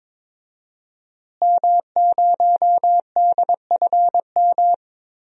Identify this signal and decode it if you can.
Morse code, heard as a keyed tone, about 21 wpm, 700 Hz. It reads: M0DFM
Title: DXCC-100 Call Signs - 5 Characters - Repeated 3x - 22wpm